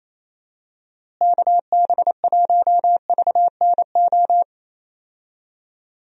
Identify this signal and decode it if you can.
Morse code, heard as a keyed tone, about 28 wpm, 700 Hz. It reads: KB1VNO
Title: USA Call Signs - 6 Characters - Repeated 3x - 28wpm